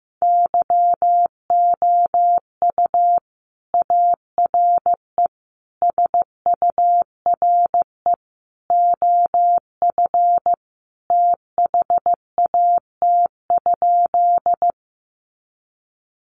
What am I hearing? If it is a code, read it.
Morse code, heard as a keyed tone, about 14 wpm, 700 Hz. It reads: YOU ARE SURE OF THAT?
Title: Sentences from Top 300 Words 15wpm